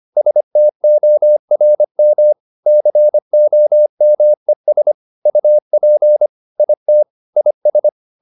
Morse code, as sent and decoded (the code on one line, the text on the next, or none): ... - --- .-. -- / -.-. --- -- . ... / ..- .--. / .. - / .. ...
STORM COMES UP IT IS